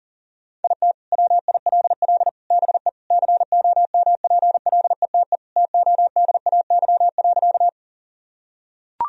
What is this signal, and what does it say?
Morse code, heard as a keyed tone, about 40 wpm, 700 Hz. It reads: IT WILL BE COMPLETE TODAY.
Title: Sentences from Top 1000 Words - Review Part 2 - 40wpm